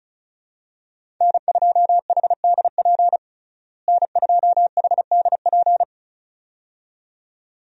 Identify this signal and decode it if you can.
Morse code, heard as a keyed tone, about 34 wpm, 700 Hz. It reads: N2HDP N2HDP
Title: USA Call Signs - 5 Characters - Repeated 2x - 35wpm